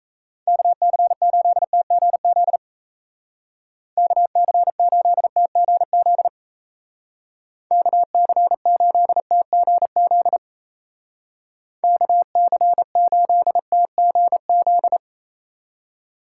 Morse code, heard as a keyed tone, about 34 wpm, 700 Hz. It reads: KC8TGZ KC8TGZ KC8TGZ KC8TGZ
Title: Speed Racing - USA 2x3 Call Signs 28wpm